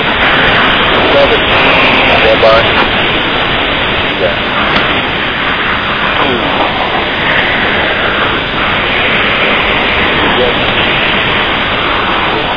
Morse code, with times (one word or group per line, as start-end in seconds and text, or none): none